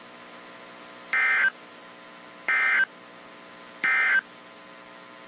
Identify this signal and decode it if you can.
Emergency Alert System (SAME) tones, heard as harsh buzzing bursts. It is heard from 1.0 to 4.3 s.
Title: WZ2523 - NOAA Weather Radio